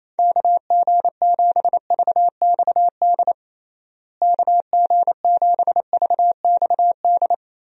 Morse code, heard as a keyed tone, about 28 wpm, 700 Hz. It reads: KG7VXD KG7VXD
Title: USA Call Signs - 6 Characters - Repeated 3x - 28wpm